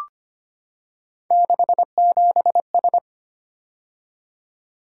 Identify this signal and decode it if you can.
Morse code, heard as a keyed tone, about 25 wpm, 700 Hz. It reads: 67S
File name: Three Letter-Number Combinations - 25wpm